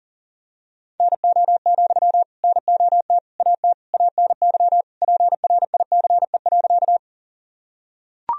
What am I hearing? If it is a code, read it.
Morse code, heard as a keyed tone, about 40 wpm, 700 Hz. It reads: NO, NOT AT ANY PRICE.
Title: Sentences from Top 1000 Words - Review Part 4 - 40wpm